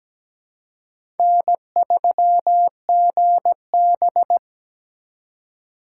Morse code, heard as a keyed tone, about 17 wpm, 700 Hz. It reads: N3GB
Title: USA Call Signs - 4 Characters - 3x Character Spacing - 17wpm